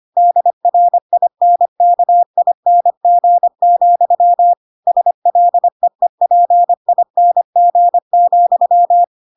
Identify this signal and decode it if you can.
Morse code, heard as a keyed tone, about 25 wpm, 700 Hz. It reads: DRINKING, SLEEPING,
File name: Pinocchio / Pinocchio ch4 25 25 ik1hge